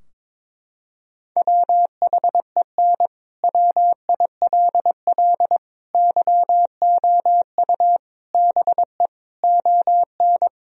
Morse code, heard as a keyed tone, about 22 wpm, 700 Hz. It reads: WHEN WILL YOU BE ON
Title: Sentences from Top 1000 Words - Review Part 2 - 22wpm